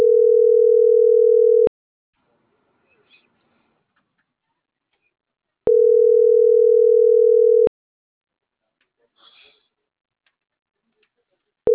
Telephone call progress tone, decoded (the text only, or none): ringback tone